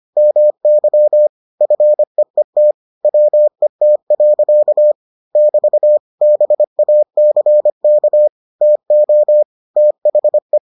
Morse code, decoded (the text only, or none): MY FEET WET. <BT> BACK TO THE